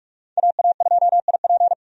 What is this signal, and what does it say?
Morse code, heard as a keyed tone, about 45 wpm, 700 Hz. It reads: AA2IP
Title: USA Call Signs - 5 Characters - 45wpm